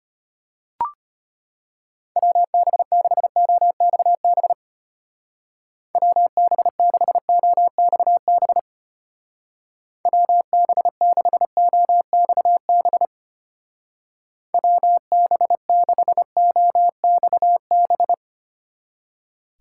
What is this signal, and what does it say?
Morse code, heard as a keyed tone, about 30 wpm, 700 Hz. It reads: WB6OXB WB6OXB WB6OXB WB6OXB
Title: Speed Racing - USA 2x3 Call Signs 25wpm